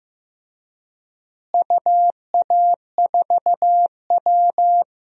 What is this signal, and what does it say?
Morse code, heard as a keyed tone, about 15 wpm, 700 Hz. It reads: UA4W
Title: DXCC-100 Call Signs - 4 Characters - Repeated 2x - 15wpm